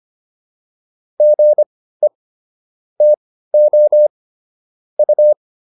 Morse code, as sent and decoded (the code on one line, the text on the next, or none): --. / - --- / ..-
GE TO U